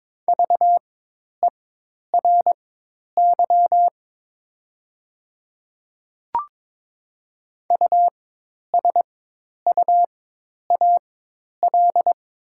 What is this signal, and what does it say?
Morse code, heard as a keyed tone, about 22 wpm, spaced as for 10 wpm, 700 Hz. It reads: VERY USUAL
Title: 4x Character Spacing - Top 500 Words - 22wpm